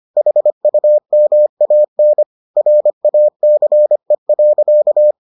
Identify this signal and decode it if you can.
Morse code, heard as a keyed tone, about 25 wpm, 600 Hz. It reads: HUMAN RACE.